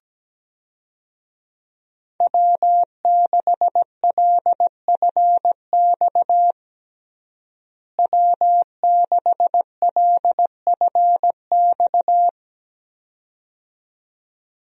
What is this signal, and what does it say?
Morse code, heard as a keyed tone, about 17 wpm, 700 Hz. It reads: W6LFX W6LFX
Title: USA Call Signs - 5 Characters - Repeated 2x - 17wpm